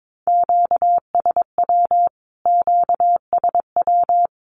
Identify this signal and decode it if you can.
Morse code, heard as a keyed tone, about 22 wpm, 700 Hz. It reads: QSW QSW
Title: Q-Codes Repeated 3x 22wpm